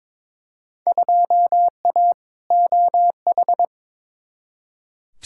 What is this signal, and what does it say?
Morse code, heard as a keyed tone, about 22 wpm, 700 Hz. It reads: 2A OH
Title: Field Day Exchanges - 22wpm